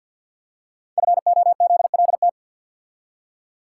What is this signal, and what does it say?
Morse code, heard as a keyed tone, about 49 wpm, 700 Hz. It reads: FOZLT
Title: Code Groups - 5 Letters - 50wpm